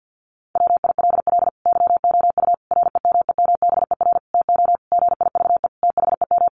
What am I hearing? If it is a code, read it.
Morse code, heard as a keyed tone, about 50 wpm, 700 Hz. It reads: WILL YOU REMEMBER TO GIVE THEM
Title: Sentences from Top 1100 Words 50wpm